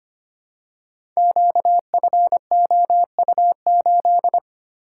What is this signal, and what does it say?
Morse code, heard as a keyed tone, about 25 wpm, 700 Hz. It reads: QFOU8
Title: Code Groups - 5 Letter:Numbers - 25wpm